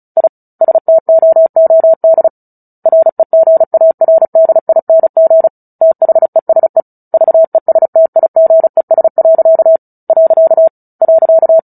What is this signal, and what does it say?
Morse code, heard as a keyed tone, about 34 wpm, 650 Hz. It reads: I STOOD REGARDING THESE VESTIGES. .
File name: Book Two / 09-Chapter 9 - 35 WPM